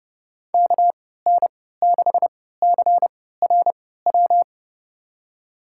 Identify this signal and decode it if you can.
Morse code, heard as a keyed tone, about 32 wpm, spaced as for 17 wpm, 700 Hz. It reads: KN6CRW
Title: USA Call Signs - 6 Characters - 3x Character Spacing - 30wpm